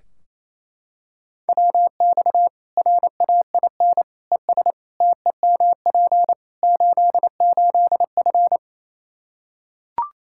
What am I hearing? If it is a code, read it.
Morse code, heard as a keyed tone, about 27 wpm, 700 Hz. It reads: WX RAIN ES TEMP 88F
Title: Practice QSO - Exchange 2 - Info - 28wpm